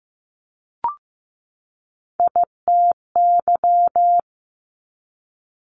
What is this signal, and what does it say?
Morse code, heard as a keyed tone, about 15 wpm, 700 Hz. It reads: ITY
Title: Top Three Letter Combinations in Words 15wpm